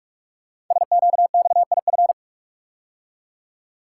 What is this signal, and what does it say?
Morse code, heard as a keyed tone, about 42 wpm, 700 Hz. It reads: SQXIF
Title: Code Groups - 5 Letter:Numbers - 45wpm